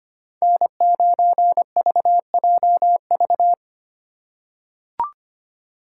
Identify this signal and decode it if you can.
Morse code, heard as a keyed tone, about 25 wpm, 700 Hz. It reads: N9VJV
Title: USA Call Signs - 5 Characters - 4x Character Spacing - 25wpm